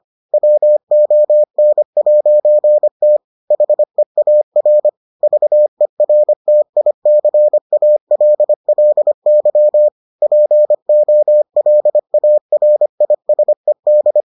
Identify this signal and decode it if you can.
Morse code, heard as a keyed tone, about 25 wpm, 600 Hz. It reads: WON'T HEAR VERTICALLY POLARISED